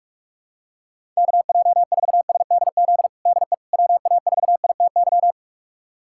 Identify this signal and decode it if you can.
Morse code, heard as a keyed tone, about 46 wpm, 700 Hz. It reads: KJ4SDZ DE WA4ITY
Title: USA Callsign DE Callsign - 6 Characters - 45wpm